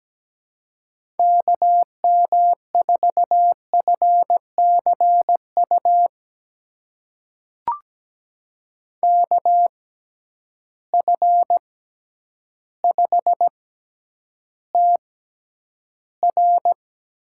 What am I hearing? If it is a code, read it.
Morse code, heard as a keyed tone, about 17 wpm, spaced as for 9 wpm, 700 Hz. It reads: KM4FCU KF5TR
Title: USA Call Signs - 6 Characters - 6x Character Spacing - 17wpm